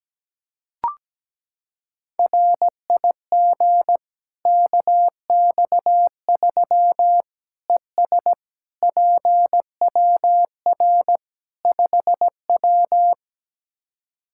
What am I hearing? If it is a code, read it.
Morse code, heard as a keyed tone, about 17 wpm, 700 Hz. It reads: RIG KX3 ES PWR 5W
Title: K1USN SST QSO Practice 17wpm